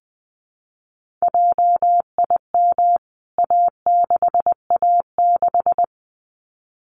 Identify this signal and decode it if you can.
Morse code, heard as a keyed tone, about 20 wpm, 700 Hz. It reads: JIM A6A6
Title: CWT Exchange 20wpm